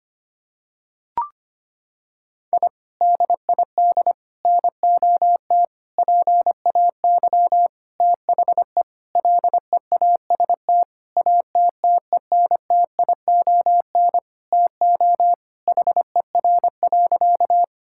Morse code, heard as a keyed tone, about 25 wpm, 700 Hz. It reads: I DID NOT PAY THE LEAST ATTENTION TO HER.
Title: Sentences from Top 1000 Words - Review Part 6 - 25wpm